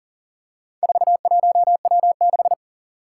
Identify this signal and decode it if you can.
Morse code, heard as a keyed tone, about 40 wpm, 700 Hz. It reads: 41W6
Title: Four Challenging Letter-Number Combinations 40wpm